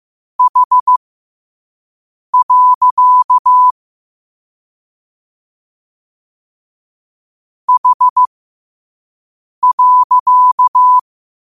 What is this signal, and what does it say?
Morse code, heard as a keyed tone, about 15 wpm, spaced as for 5 wpm, 1000 Hz. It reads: H. H.
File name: qotdfiles / 5wpm